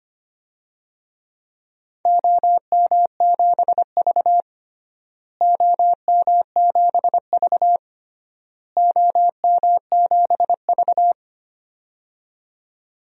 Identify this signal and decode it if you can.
Morse code, heard as a keyed tone, about 25 wpm, 700 Hz. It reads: OM7V OM7V OM7V
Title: DXCC-100 Call Signs - 4 Characters - Repeated 3x - 25wpm